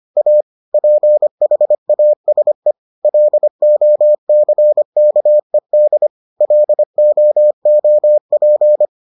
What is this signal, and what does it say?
Morse code, heard as a keyed tone, about 25 wpm, 600 Hz. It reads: A PHASE LOCKED LOOP